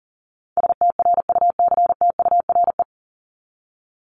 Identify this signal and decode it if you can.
Morse code, heard as a keyed tone, about 40 wpm, 700 Hz. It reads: STRUCTURE
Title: Top 751-1000 Words 40wpm